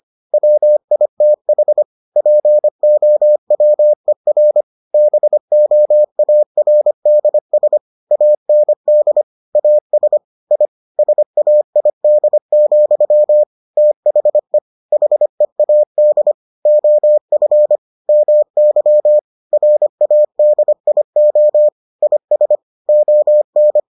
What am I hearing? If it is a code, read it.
Morse code, heard as a keyed tone, about 25 wpm, 600 Hz. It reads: WITH POWER BOARDS AND AS I SAID, THE HEAD OF MY RADIO IS ON